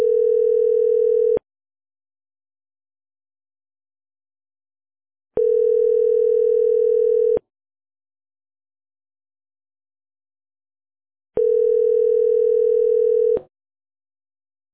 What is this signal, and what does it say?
Ringback tone (North America), 440 + 480 Hz.